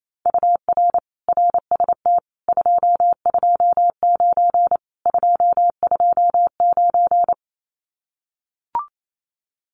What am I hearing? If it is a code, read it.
Morse code, heard as a keyed tone, about 28 wpm, 700 Hz. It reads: UR RST 229 229
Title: Common QSO Phrases Encore 28wpm